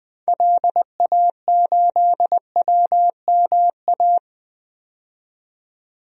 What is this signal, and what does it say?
Morse code, heard as a keyed tone, about 20 wpm, 700 Hz. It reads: LA8WMA
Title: DXCC-100 Call Signs - 6 Characters - 20wpm